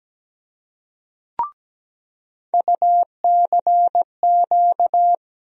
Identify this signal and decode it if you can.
Morse code, heard as a keyed tone, about 17 wpm, 700 Hz. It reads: UCQ
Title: USA Call Sign Postfixes - 3 Letters 17wpm